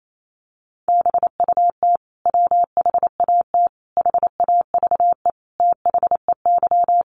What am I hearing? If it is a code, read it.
Morse code, heard as a keyed tone, about 28 wpm, 700 Hz. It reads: BUT WHAT HAVE THEY
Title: Sentences from Top 100 Words 28wpm